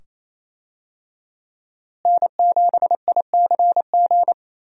Morse code, heard as a keyed tone, about 28 wpm, 700 Hz. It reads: N7ICG
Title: USA Call Signs - 5 Characters - Repeated 2x - 28wpm